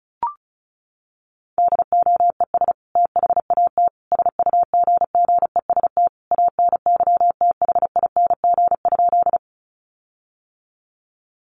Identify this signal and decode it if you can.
Morse code, heard as a keyed tone, about 34 wpm, 700 Hz. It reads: DOES THAT SUGGEST ANYTHING?
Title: Sentences from Top 1300 Words 35wpm